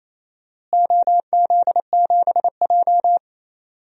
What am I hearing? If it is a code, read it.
Morse code, heard as a keyed tone, about 28 wpm, 700 Hz. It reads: OZ7J
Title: DXCC-100 Call Signs - 4 Characters - Repeated 2x - 28wpm